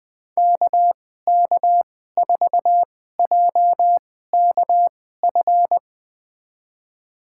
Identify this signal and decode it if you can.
Morse code, heard as a keyed tone, about 20 wpm, spaced as for 14 wpm, 700 Hz. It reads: KK4JKF
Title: USA Call Signs - 6 Characters - 2x Character Spacing - 20wpm